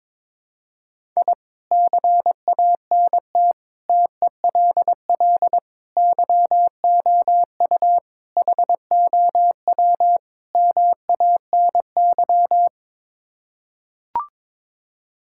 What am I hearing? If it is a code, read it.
Morse code, heard as a keyed tone, about 22 wpm, 700 Hz. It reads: I CANT TELL YOU HOW MANY
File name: practice / 22wpm / Sets of 6 Words - 22wpm